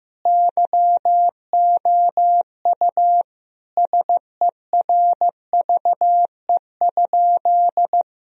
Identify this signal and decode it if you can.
Morse code, heard as a keyed tone, about 15 wpm, 700 Hz. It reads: YOU SERVE?
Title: Sentences from Top 1000 Words - Review Part 2 - 15wpm